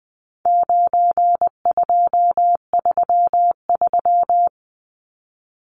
Morse code, heard as a keyed tone, about 20 wpm, 700 Hz. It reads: 9233
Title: Numbers - 4-Digit 20wpm